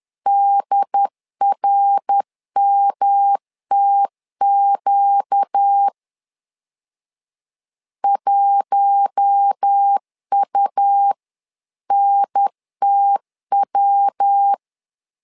Morse code, heard as a keyed tone, about 10 wpm, 800 Hz. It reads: DRMTQ 1UNTW
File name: LEZIONI / 4 Q1 10wpm 2448